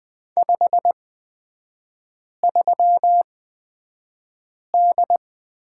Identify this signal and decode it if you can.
Morse code, heard as a keyed tone, about 20 wpm, spaced as for 5 wpm, 690 Hz. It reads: 53D
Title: 229 Lesson 37e - Letter x - Three Characters 20wpm